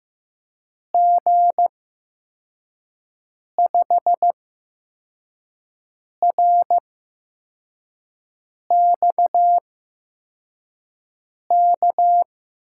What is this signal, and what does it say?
Morse code, heard as a keyed tone, about 15 wpm, spaced as for 4 wpm, 700 Hz. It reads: G5RXK
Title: USA Call Signs - 6 Characters - 8x Character Spacing - 15wpm